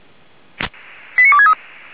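Five-tone selective call (ZVEI), heard as quick runs of five tones, tones of about 70 ms. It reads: 89263
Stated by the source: Five-tone ZVEI selective call 89248 received on an SDR.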